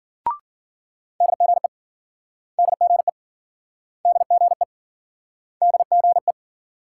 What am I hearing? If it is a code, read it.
Morse code, heard as a keyed tone, about 56 wpm, 700 Hz. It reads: DGE DGE DGE DGE